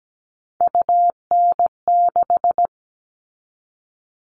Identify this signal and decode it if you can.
Morse code, heard as a keyed tone, about 17 wpm, 700 Hz. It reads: UN6
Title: All Cut Numbers - 3-Digit 17wpm